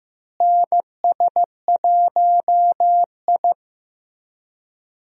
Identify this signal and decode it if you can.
Morse code, heard as a keyed tone, about 15 wpm, 700 Hz. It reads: NS1I
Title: USA Call Signs - 4 Characters - Repeated 3x - 15wpm